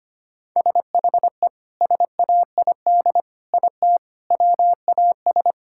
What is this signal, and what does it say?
Morse code, heard as a keyed tone, about 25 wpm, 700 Hz. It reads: SHE SAID IT WAS
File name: practice / 25wpm / Sentences with 2-Word Phrases - 25wpm